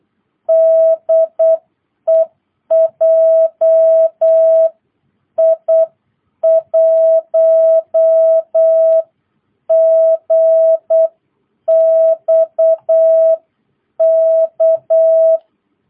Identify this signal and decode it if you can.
Morse code, heard as a keyed tone, about 8 wpm, spaced as for 7 wpm, 660 Hz. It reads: DE JI1GXK